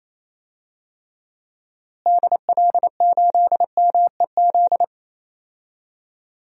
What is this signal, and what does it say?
Morse code, heard as a keyed tone, about 28 wpm, 700 Hz. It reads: DL8MEZ
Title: DXCC-100 Call Signs - 6 Characters - Repeated 2x - 28wpm